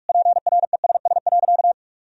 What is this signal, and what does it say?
Morse code, heard as a keyed tone, about 41 wpm, 700 Hz. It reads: PRESS.
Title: Sentences from Top 1000 Words - Review Part 1 - 45wpm